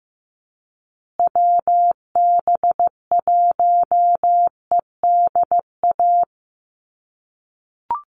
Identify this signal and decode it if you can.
Morse code, heard as a keyed tone, about 15 wpm, 700 Hz. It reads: WB1EDA
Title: USA Call Signs Repeated 2x 15wpm